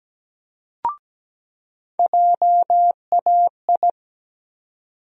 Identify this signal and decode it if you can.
Morse code, heard as a keyed tone, about 17 wpm, 700 Hz. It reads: JAI
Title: USA Call Sign Postfixes - 3 Letters 17wpm